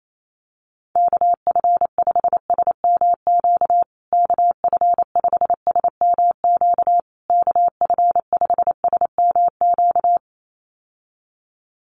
Morse code, heard as a keyed tone, about 28 wpm, 700 Hz. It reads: KF5SMQ KF5SMQ KF5SMQ
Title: USA Call Signs Repeated 3x 28wpm